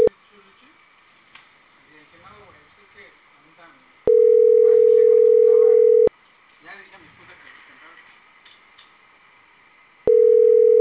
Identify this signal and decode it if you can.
Ringback tone (North America), 440 + 480 Hz, with speech.
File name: CENTRAL CAEA / out agcaea02 - .49247